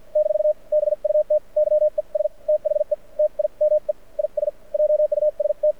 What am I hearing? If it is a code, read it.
Morse code, heard as a keyed tone, about 53 wpm, spaced as for 43 wpm, 600 Hz. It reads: - BUT YES THE TIME IS JUST
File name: archived / CRSnet